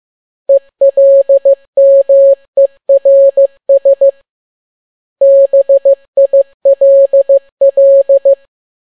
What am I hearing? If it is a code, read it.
Morse code, heard as a keyed tone, about 15 wpm, 550 Hz. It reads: ELMERS BILL
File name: n6tt p1 15C13